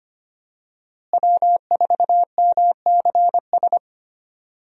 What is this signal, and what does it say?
Morse code, heard as a keyed tone, about 25 wpm, 700 Hz. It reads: W4MCS
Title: USA Call Signs - 5 Characters - Repeated 3x - 25wpm